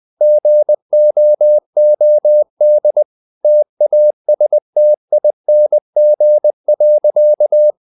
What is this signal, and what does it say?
Morse code, heard as a keyed tone, about 20 wpm, 600 Hz. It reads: GOOD TASTING.
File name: the old man and the sea / the old man and the sea part 1 20wpm